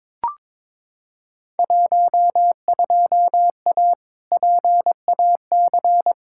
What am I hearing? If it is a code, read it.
Morse code, heard as a keyed tone, about 22 wpm, 700 Hz. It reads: 12A PAC
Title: Field Day Exchanges Encore - 22wpm